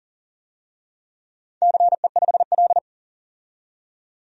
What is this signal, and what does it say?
Morse code, heard as a keyed tone, about 40 wpm, 690 Hz. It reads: CE5L
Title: DXCC-100 Call Signs - 4 Characters - 40wpm